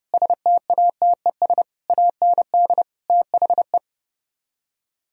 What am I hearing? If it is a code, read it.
Morse code, heard as a keyed tone, about 39 wpm, 700 Hz. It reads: STATES AND THE